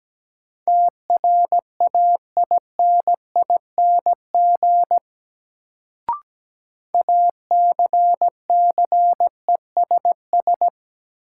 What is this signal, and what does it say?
Morse code, heard as a keyed tone, about 17 wpm, 700 Hz. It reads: TRAINING ACCESS
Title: Sentences from Top 1000 Words - Review Part 4 - New Word Drill - 17wpm